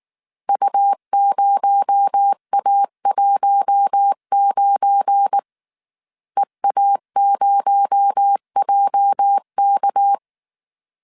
Morse code, heard as a keyed tone, about 19 wpm, 800 Hz. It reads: U0A19 EA0JK
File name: CW / DET / 1 9 1500C S A 18wpm 2459